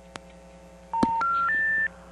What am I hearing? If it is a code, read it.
Special information tone.
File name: TurtleCreekEWSD